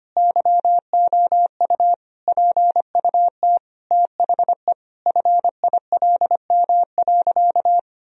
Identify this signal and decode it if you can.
Morse code, heard as a keyed tone, about 25 wpm, 700 Hz. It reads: YOU PUT THE FILM.
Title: Sentences from Top 1000 Words - Review Part 3 - 25wpm